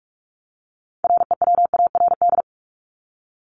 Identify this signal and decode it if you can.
Morse code, heard as a keyed tone, about 44 wpm, 700 Hz. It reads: REWARD
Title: Top 1751-2000 Words 45wpm